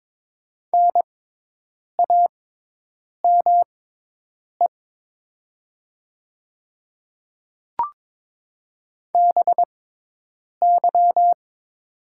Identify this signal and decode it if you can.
Morse code, heard as a keyed tone, about 22 wpm, spaced as for 8 wpm, 700 Hz. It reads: NAME BY
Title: 6x Character Spacing - Top 200 Words - 22wpm